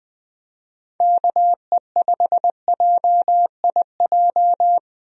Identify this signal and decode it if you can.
Morse code, heard as a keyed tone, about 20 wpm, 700 Hz. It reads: KE5JIJ
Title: USA Call Signs - 6 Characters - Repeated 3x - 20wpm